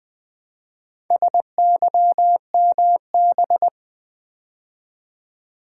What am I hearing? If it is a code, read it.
Morse code, heard as a keyed tone, about 20 wpm, 690 Hz. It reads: SYMB
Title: Build A Word - 2 Letters At A Time - 6-Letter Words - 20wpm